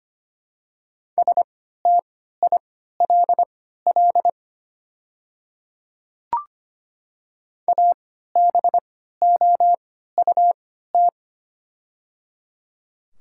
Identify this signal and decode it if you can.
Morse code, heard as a keyed tone, about 25 wpm, spaced as for 14 wpm, 700 Hz. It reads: STILL ABOUT
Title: 3x Character Spacing - Top 300 Words - 25wpm